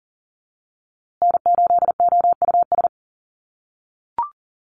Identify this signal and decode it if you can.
Morse code, heard as a keyed tone, about 40 wpm, 700 Hz. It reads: N8OUS
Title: USA Call Signs Repeated 3x 40wpm